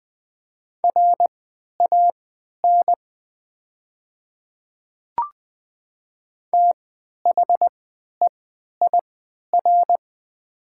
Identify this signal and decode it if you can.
Morse code, heard as a keyed tone, about 20 wpm, spaced as for 11 wpm, 700 Hz. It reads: RAN THEIR
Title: 3x Character Spacing - Top 500 Words - 20wpm